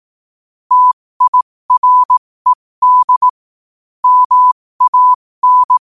Morse code, heard as a keyed tone, about 18 wpm, spaced as for 15 wpm, 1000 Hz. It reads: TIRED MAN
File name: qotdfiles / 15wpm